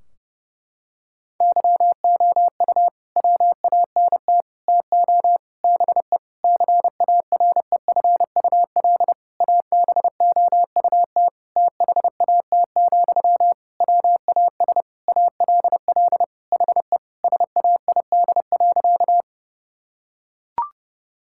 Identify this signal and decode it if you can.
Morse code, heard as a keyed tone, about 30 wpm, 700 Hz. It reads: YOU WANT TO BE CAREFUL ABOUT THAT, WAS ALL HE SAID.
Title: Sentences from Top 1600 Words 30wpm